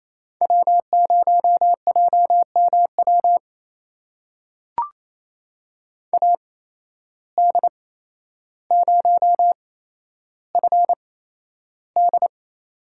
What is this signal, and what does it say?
Morse code, heard as a keyed tone, about 28 wpm, spaced as for 12 wpm, 700 Hz. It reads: W0JMW AD0FD
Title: USA Call Signs - 5 Characters - 8x Character Spacing - 28wpm